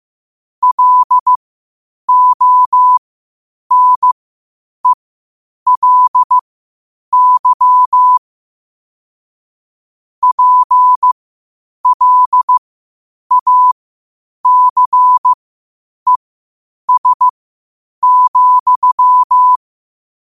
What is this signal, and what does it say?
Morse code, heard as a keyed tone, about 15 wpm, spaced as for 8 wpm, 1000 Hz. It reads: LONELY PLACES,
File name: qotdfiles / 8wpm